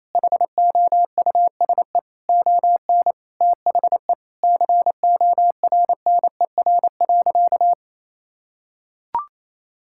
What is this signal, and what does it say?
Morse code, heard as a keyed tone, about 28 wpm, 700 Hz. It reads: HOUSE ON THE CORNER.
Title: Sentences from Top 1600 Words 28wpm